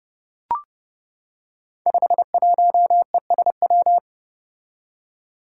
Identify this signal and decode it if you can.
Morse code, heard as a keyed tone, about 30 wpm, 700 Hz. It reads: 51ESW
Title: Five Challenging Letter-Number Combinations 30wpm